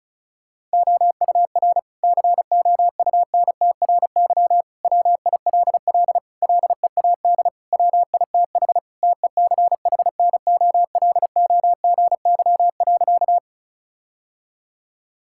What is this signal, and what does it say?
Morse code, heard as a keyed tone, about 34 wpm, 700 Hz. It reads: OUR COUNTRY WILL LEAD WITH TECHNOLOGY.
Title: Sentences from Top 1000 Words - Review Part 5 - 35wpm